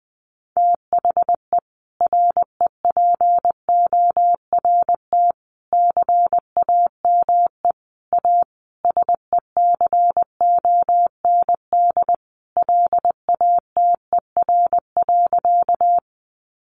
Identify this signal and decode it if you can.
Morse code, heard as a keyed tone, about 20 wpm, 700 Hz. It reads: THE REPORT CAME A SECOND LATER.
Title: Sentences from Top 900 Words 20wpm